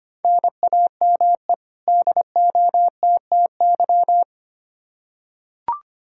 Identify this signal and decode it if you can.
Morse code, heard as a keyed tone, about 25 wpm, 700 Hz. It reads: NAME DOTTY
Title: K1USN SST QSO Practice 25wpm